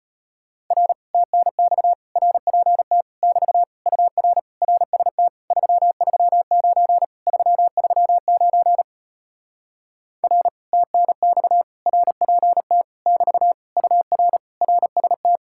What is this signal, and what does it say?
Morse code, heard as a keyed tone, about 33 wpm, 700 Hz. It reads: R TNX RPT <BT> UR RST 339 339 R TNX RPT <BT> UR RST